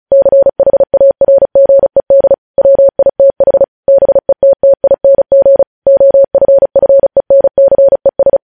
Morse code, heard as a keyed tone, about 36 wpm, 550 Hz. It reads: CHARGED WITH BETTING OFFENCES
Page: news headlines in morse code at 35 wpm